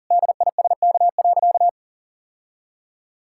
Morse code, heard as a keyed tone, about 40 wpm, 700 Hz. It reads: DISK.
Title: Sentences from Top 1000 Words - Review Part 6 - 40wpm